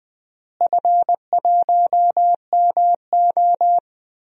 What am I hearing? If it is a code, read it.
Morse code, heard as a keyed tone, about 20 wpm, 700 Hz. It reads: F1MO
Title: DXCC-100 Call Signs - 4 Characters - Repeated 3x - 20wpm